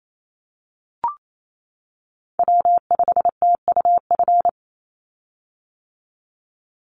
Morse code, heard as a keyed tone, about 28 wpm, 700 Hz. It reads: W5TUF
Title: Call Signs - 28wpm